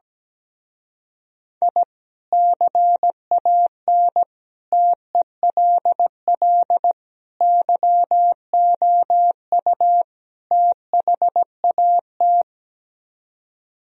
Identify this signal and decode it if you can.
Morse code, heard as a keyed tone, about 17 wpm, 700 Hz. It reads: I CAN TELL YOU THAT
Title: Speed-Racing - Sets of 5 Words 17wpm